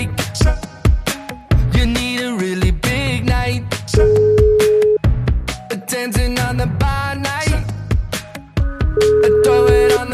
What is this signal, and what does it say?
Ringback tone (Europe, 425 Hz), with noise.